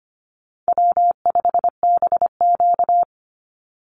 Morse code, heard as a keyed tone, about 25 wpm, 700 Hz. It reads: W5BQ
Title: Easy Call Signs Repeated 3x - 25wpm